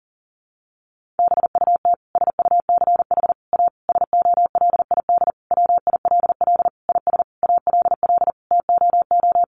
Morse code, heard as a keyed tone, about 40 wpm, 700 Hz. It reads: BUT SUCH A SOLID WILL IS ALL TOO
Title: Sentences from Top 1500 Words 40wpm